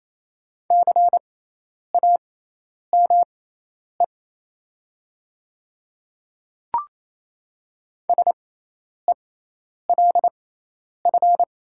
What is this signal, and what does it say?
Morse code, heard as a keyed tone, about 27 wpm, spaced as for 10 wpm, 700 Hz. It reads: CAME SELF
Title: 6x Character Spacing - Top 200 Words - 28wpm